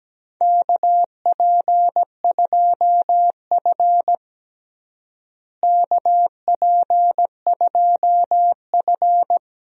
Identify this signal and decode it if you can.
Morse code, heard as a keyed tone, about 17 wpm, 700 Hz. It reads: KP2F KP2F
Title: DXCC-100 Call Signs - 4 Characters - Repeated 2x - 17wpm